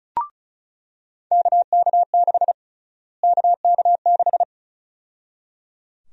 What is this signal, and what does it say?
Morse code, heard as a keyed tone, about 34 wpm, 700 Hz. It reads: KK6 KK6
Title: USA Call Sign Prefixes - Repeated 2x - 35wpm